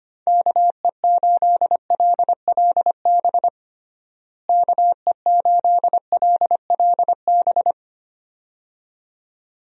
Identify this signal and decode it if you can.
Morse code, heard as a keyed tone, about 25 wpm, 700 Hz. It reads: KE8LLB KE8LLB
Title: USA Call Signs - 6 Characters - Repeated 2x - 25wpm